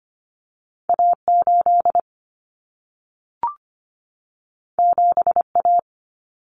Morse code, heard as a keyed tone, about 25 wpm, 700 Hz. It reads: A8 7A
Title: Common Cut Numbers - 2-Digit 25wpm